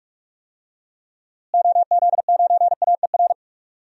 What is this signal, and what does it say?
Morse code, heard as a keyed tone, about 46 wpm, 700 Hz. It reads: OZ9AER
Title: DXCC-100 Call Signs - 6 Characters - Repeated 2x - 45wpm